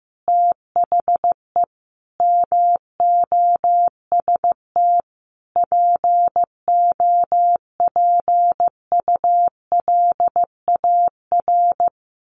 Morse code, heard as a keyed tone, about 14 wpm, 700 Hz. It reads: THE MOST POPULAR
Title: Sets of 5 Words 15wpm